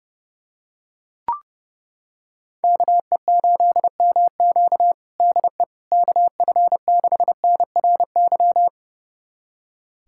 Morse code, heard as a keyed tone, about 32 wpm, 700 Hz. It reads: KE8MQ DE KF6NRY
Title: USA Callsign DE Callsign 30wpm